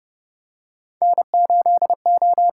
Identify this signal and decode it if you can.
Morse code, heard as a keyed tone, about 30 wpm, 700 Hz. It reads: N8O
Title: USA Call Signs - 6x Character Spacing - 30wpm